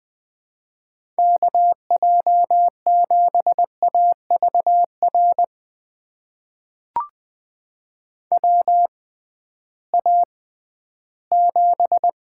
Morse code, heard as a keyed tone, about 21 wpm, spaced as for 13 wpm, 700 Hz. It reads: KJ7AVR WA7
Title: USA Call Signs - 6x Character Spacing - 20wpm